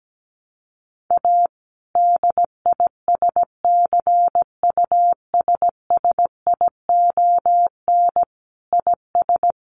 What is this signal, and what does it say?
Morse code, heard as a keyed tone, about 17 wpm, 700 Hz. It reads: A DISCUSSION IS
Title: Sentences from Top 1100 Words 17wpm